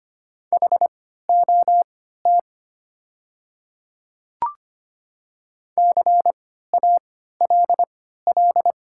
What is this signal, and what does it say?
Morse code, heard as a keyed tone, about 25 wpm, spaced as for 14 wpm, 700 Hz. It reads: HOT CALL
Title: 3x Character Spacing - Top 100 Words - 25wpm